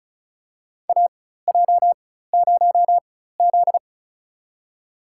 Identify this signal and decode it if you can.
Morse code, heard as a keyed tone, about 36 wpm, spaced as for 16 wpm, 700 Hz. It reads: AJ0Z
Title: USA Call Signs - 4 Characters - 4x Character Spacing - 35wpm